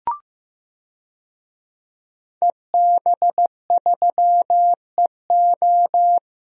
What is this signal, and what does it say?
Morse code, heard as a keyed tone, about 15 wpm, 700 Hz. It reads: EB3EO
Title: DXCC-100 Call Signs - 5 Characters - Repeated 2x - 15wpm